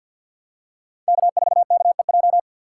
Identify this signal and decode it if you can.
Morse code, heard as a keyed tone, about 52 wpm, 700 Hz. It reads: K4KEJ
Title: USA Call Signs - 5 Characters - 6x Character Spacing - 50wpm